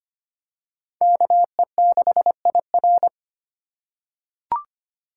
Morse code, heard as a keyed tone, about 25 wpm, 700 Hz. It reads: KE6IR
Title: USA Call Signs - 5 Characters - 4x Character Spacing - 25wpm